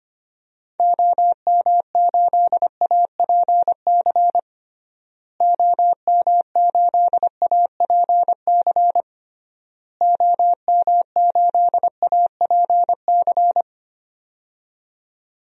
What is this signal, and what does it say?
Morse code, heard as a keyed tone, about 25 wpm, 700 Hz. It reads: OM8APC OM8APC OM8APC
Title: DXCC-100 Call Signs - 6 Characters - Repeated 3x - 25wpm